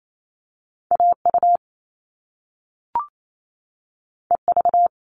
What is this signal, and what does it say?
Morse code, heard as a keyed tone, about 28 wpm, 700 Hz. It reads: AU EV
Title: Top Two Letter Combinations in Words 28wpm